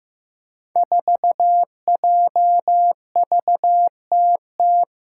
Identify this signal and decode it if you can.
Morse code, heard as a keyed tone, about 15 wpm, 700 Hz. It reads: 4JVTT
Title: Five Challenging Letter-Number Combinations 15wpm